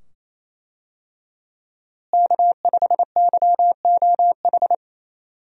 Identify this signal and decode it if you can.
Morse code, heard as a keyed tone, about 27 wpm, 700 Hz. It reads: K5YOH
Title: USA Call Signs - 5 Characters - Repeated 2x - 28wpm